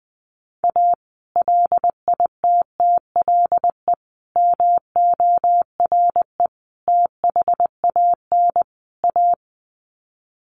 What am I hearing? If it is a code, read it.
Morse code, heard as a keyed tone, about 20 wpm, 700 Hz. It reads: A LITTLE MORE THAN A
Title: Sets of 5 Words Encore2 20wpm